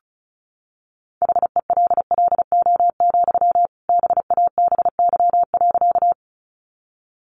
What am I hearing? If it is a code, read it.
Morse code, heard as a keyed tone, about 34 wpm, 700 Hz. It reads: HELLO, BABY.
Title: Sentences from Top 1300 Words 35wpm